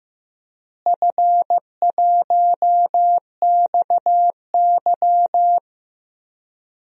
Morse code, heard as a keyed tone, about 15 wpm, 700 Hz. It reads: F1XY
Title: DXCC-100 Call Signs - 4 Characters - Repeated 2x - 15wpm